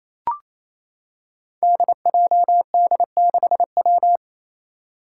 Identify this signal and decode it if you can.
Morse code, heard as a keyed tone, about 28 wpm, 700 Hz. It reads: DJD6W
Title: Five Challenging Letter-Number Combinations 28wpm